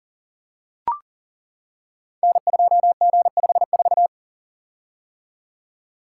Morse code, heard as a keyed tone, about 38 wpm, 700 Hz. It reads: N2G54
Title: Code Groups - 5 Letter:Numbers - 40wpm